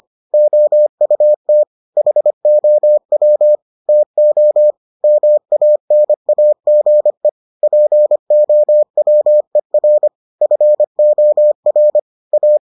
Morse code, heard as a keyed tone, about 25 wpm, 600 Hz. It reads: OUT HOW TO MANAGE POWER FOR A